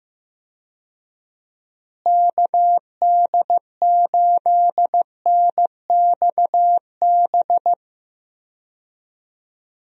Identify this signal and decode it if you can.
Morse code, heard as a keyed tone, about 15 wpm, 700 Hz. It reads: KD8NXB
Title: USA Call Signs - 6 Characters - Repeated 3x - 15wpm